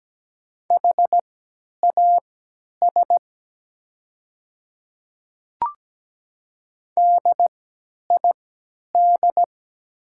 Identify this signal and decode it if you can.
Morse code, heard as a keyed tone, about 17 wpm, spaced as for 10 wpm, 690 Hz. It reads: HAS DID
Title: 3x Character Spacing - Top 100 Words - 17wpm